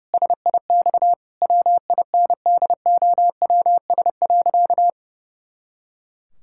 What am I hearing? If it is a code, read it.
Morse code, heard as a keyed tone, about 30 wpm, 700 Hz. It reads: SIX WINDOWS.
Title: Pangram Challenge v2 - 30wpm